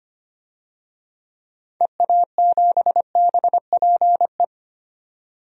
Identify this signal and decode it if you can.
Morse code, heard as a keyed tone, about 25 wpm, 700 Hz. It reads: EA7BPE
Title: DXCC-100 Call Signs - 6 Characters - 25wpm